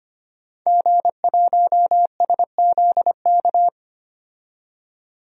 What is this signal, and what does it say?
Morse code, heard as a keyed tone, about 25 wpm, 700 Hz. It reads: G1SZK
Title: DXCC-100 Call Signs - 5 Characters - 25wpm